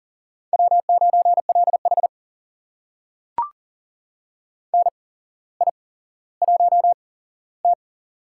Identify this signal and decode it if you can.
Morse code, heard as a keyed tone, about 39 wpm, spaced as for 17 wpm, 700 Hz. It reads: W9LH NI1T
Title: USA Call Signs - 4 Characters - 8x Character Spacing - 40wpm